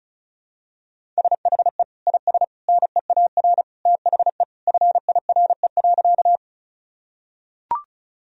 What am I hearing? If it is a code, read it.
Morse code, heard as a keyed tone, about 34 wpm, 700 Hz. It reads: SHE IS NEAR THE FIRE.
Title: Sentences from Top 1000 Words - Review Part 1 - 35wpm